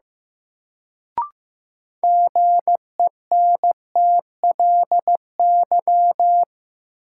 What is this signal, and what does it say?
Morse code, heard as a keyed tone, about 15 wpm, 700 Hz. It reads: GENTLY
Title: Sentences from Top 1500 Words - New Word Drill Alt - 15wpm